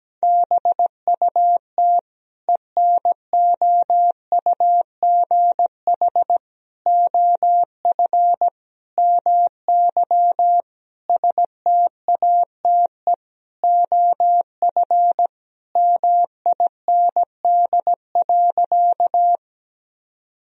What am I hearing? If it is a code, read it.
Morse code, heard as a keyed tone, about 17 wpm, 700 Hz. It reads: BUT ENOUGH OF MY STATE OF MIND.
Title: Sentences from Top 1000 Words - Review Part 2 - 17wpm